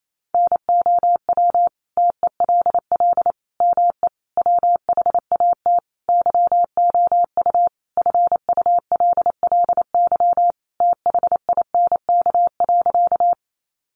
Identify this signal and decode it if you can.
Morse code, heard as a keyed tone, about 28 wpm, 700 Hz. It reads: NOW TELL ME WHAT YOU FULLY THINK.
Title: Sentences from Top 1300 Words 28wpm